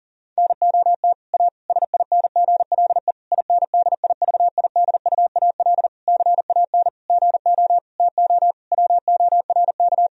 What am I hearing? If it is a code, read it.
Morse code, heard as a keyed tone, about 37 wpm, 700 Hz. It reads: NOT A SINGLE INDIVIDUAL CAN GO TO WORK
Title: Sentences from Top 1000 Words - Review Part 4 - 40wpm